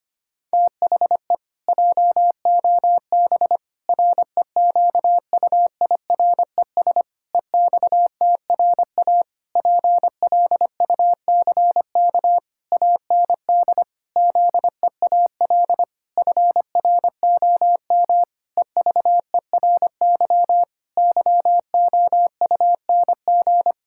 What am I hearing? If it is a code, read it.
Morse code, heard as a keyed tone, about 25 wpm, 700 Hz. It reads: THE JOB REQUIRES EXTRA PLUCK AND ZEAL FROM EVERY YOUNG
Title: Pangram Challenge v2 - 25wpm